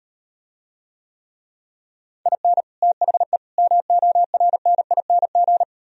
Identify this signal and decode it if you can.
Morse code, heard as a keyed tone, about 38 wpm, 700 Hz. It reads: IN THE MORNING